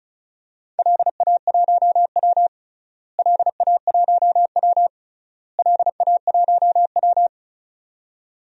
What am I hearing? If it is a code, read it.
Morse code, heard as a keyed tone, about 34 wpm, 700 Hz. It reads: LA1W LA1W LA1W
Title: DXCC-100 Call Signs - 4 Characters - Repeated 3x - 35wpm